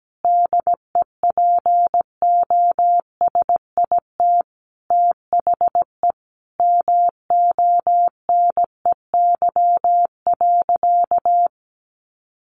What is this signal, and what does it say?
Morse code, heard as a keyed tone, about 17 wpm, 700 Hz. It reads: DEPOSIT THE MONEY.
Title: Sentences from Top 1300 Words 17wpm